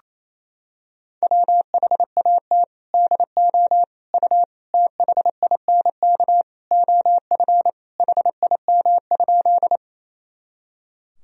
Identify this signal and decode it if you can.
Morse code, heard as a keyed tone, about 28 wpm, 700 Hz. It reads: WHAT DO U THINK OF HIM?
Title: Sentences from Top 200 Words and Abbreviations 28wpm